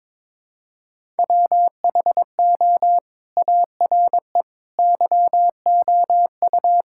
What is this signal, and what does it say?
Morse code, heard as a keyed tone, about 22 wpm, 700 Hz. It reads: WHO ARE YOU
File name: practice / 22wpm / Sentences with 2-Word Phrases - 22wpm